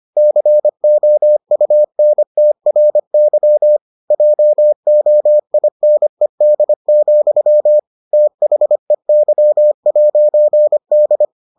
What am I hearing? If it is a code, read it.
Morse code, heard as a keyed tone, about 25 wpm, 600 Hz. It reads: COUNTRY JOINED, THEY'D